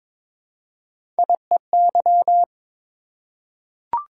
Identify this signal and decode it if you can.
Morse code, heard as a keyed tone, about 22 wpm, 700 Hz. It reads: IEY
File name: practice / 22wpm / Three Letter-Number Combinations - 22wpm